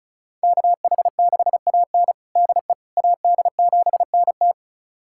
Morse code, heard as a keyed tone, about 36 wpm, 700 Hz. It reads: KH6AN DE AD7NT
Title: USA Callsign DE Callsign 35wpm